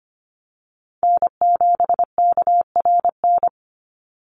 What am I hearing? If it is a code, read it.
Morse code, heard as a keyed tone, about 25 wpm, 700 Hz. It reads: N7KRN
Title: NA SOTA Chaser Callsigns 25wpm